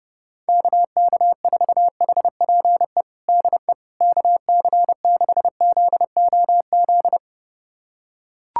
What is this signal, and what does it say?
Morse code, heard as a keyed tone, about 30 wpm, 700 Hz. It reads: KK4HPE DE KC6ZOZ
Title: USA Callsign DE Callsign - 6 Characters - 30wpm